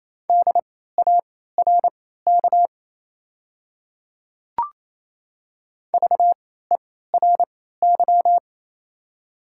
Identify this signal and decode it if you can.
Morse code, heard as a keyed tone, about 28 wpm, spaced as for 16 wpm, 690 Hz. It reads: DARK VERY
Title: 3x Character Spacing - Top 500 Words - 28wpm